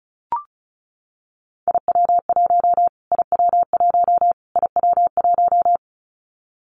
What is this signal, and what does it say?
Morse code, heard as a keyed tone, about 34 wpm, 700 Hz. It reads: IW1 IW1 IW1
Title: Callsign Prefixes for Top 25 DXCC - Repeated 3x - 35wpm